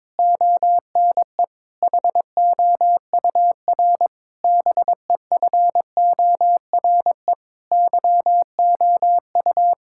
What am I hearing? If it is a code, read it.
Morse code, heard as a keyed tone, about 23 wpm, 700 Hz. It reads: ONE HOUR BEFORE YOU
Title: Sentences from Top 1000 Words - Review Part 6 - 22wpm